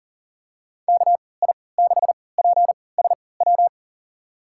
Morse code, heard as a keyed tone, about 40 wpm, spaced as for 23 wpm, 700 Hz. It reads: KI6PSW
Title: USA Call Signs - 6 Characters - 3x Character Spacing - 40wpm